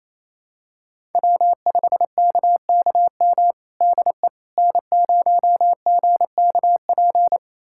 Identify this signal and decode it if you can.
Morse code, heard as a keyed tone, about 28 wpm, 700 Hz. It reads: W5KKM DE N0GKP
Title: USA Callsign DE Callsign - 5 Characters - 28wpm